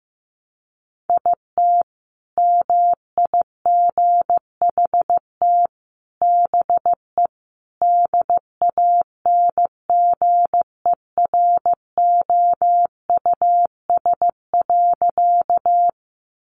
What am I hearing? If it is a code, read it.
Morse code, heard as a keyed tone, about 15 wpm, 700 Hz. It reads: IT MIGHT BE DANGEROUS.
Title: Sentences from Top 1200 Words 15wpm